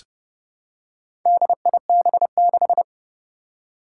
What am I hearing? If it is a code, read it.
Morse code, heard as a keyed tone, about 31 wpm, 700 Hz. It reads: DIB6
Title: Four Challenging Letter-Number Combinations 30wpm